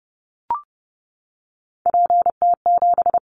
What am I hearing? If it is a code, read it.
Morse code, heard as a keyed tone, about 30 wpm, 700 Hz. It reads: PT7
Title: Callsign Prefixes for Top 100 DXCC - 30wpm